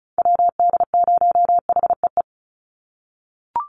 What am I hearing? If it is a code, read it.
Morse code, heard as a keyed tone, about 34 wpm, 700 Hz. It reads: WD0HEE
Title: USA Call Signs 35wpm